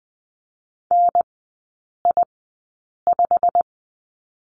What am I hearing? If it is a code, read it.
Morse code, heard as a keyed tone, about 19 wpm, spaced as for 8 wpm, 700 Hz. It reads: NI5
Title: 087 Lesson 15e - Number 5 - Three Characters 20wpm